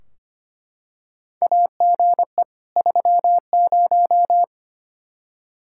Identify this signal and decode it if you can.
Morse code, heard as a keyed tone, about 25 wpm, 700 Hz. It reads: AGE 30
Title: Practice QSO - Exchange 3 - More Info - 25wpm